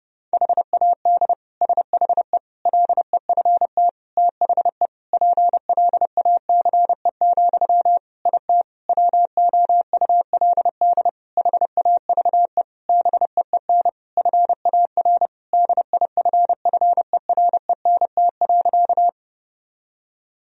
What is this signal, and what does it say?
Morse code, heard as a keyed tone, about 32 wpm, 700 Hz. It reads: HAD SHE LEFT THE PLACE, IT WOULD HAVE BEEN FAR DIFFERENT.
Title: Sentences from Top 1000 Words - Review Part 3 - 30wpm